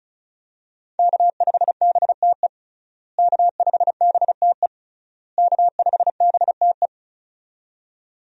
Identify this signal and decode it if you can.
Morse code, heard as a keyed tone, about 36 wpm, 700 Hz. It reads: K5BTE K5BTE K5BTE
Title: USA Call Signs - 5 Characters - Repeated 3x - 35wpm